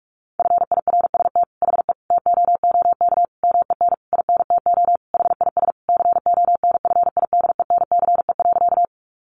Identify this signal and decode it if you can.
Morse code, heard as a keyed tone, about 44 wpm, 700 Hz. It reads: FIRST HE TOOK MEN INTO HIS CONFIDENCE.
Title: Sentences from Top 1500 Words 45wpm